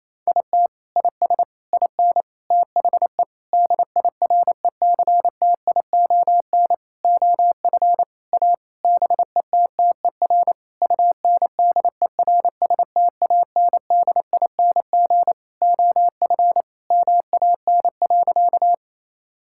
Morse code, heard as a keyed tone, about 28 wpm, 700 Hz. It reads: IT IS IN THE DIRECTION OF A BETTER UNDERSTANDING OF MAN.
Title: Sentences from Top 1000 Words - Review Part 5 - 28wpm